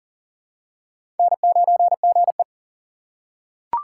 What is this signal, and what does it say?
Morse code, heard as a keyed tone, about 40 wpm, 700 Hz. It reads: N9GE
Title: USA Call Signs - 4 Characters - 4x Character Spacing - 40wpm